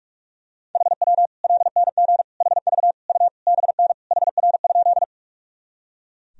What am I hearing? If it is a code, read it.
Morse code, heard as a keyed tone, about 41 wpm, 700 Hz. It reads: HW LNG HV U BN HR?
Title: Sentences from Top 200 Words and Abbreviations 45wpm